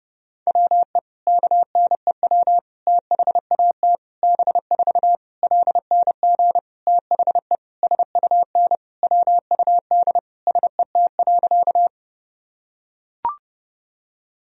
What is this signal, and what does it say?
Morse code, heard as a keyed tone, about 30 wpm, 700 Hz. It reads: WE KNEW THAT B4 LNG THE SUN WUD SET.
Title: Sentences from Top 400 Words and Abbreviations 30wpm